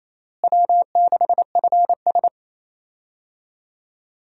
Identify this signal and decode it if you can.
Morse code, heard as a keyed tone, about 27 wpm, 700 Hz. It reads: W6FS
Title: USA Call Signs - 4 Characters - Repeated 2x - 28wpm